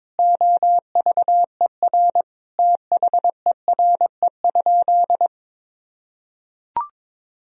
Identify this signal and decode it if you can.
Morse code, heard as a keyed tone, about 22 wpm, 700 Hz. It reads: OVER THERE?
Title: Sentences from Top 1000 Words - Review Part 2 - 22wpm